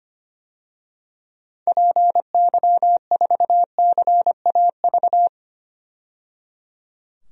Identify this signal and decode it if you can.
Morse code, heard as a keyed tone, about 25 wpm, 700 Hz. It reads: PY4CAV
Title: DXCC-100 Call Signs - 6 Characters - 25wpm